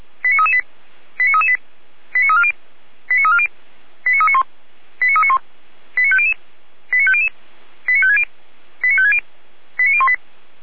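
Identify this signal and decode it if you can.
Five-tone selective call (ZVEI), heard as quick runs of five tones, tones of about 70 ms. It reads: 89338 89338 89340 89340 89391 89391 89500 89500 89560 89560 89018
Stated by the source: ZVEI five-tone selective call received by SDR.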